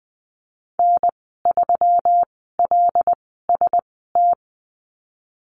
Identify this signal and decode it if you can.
Morse code, heard as a keyed tone, about 20 wpm, spaced as for 15 wpm, 700 Hz. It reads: N3LST
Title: 147 Lesson 24g - Letter y - Call Signs 20wpm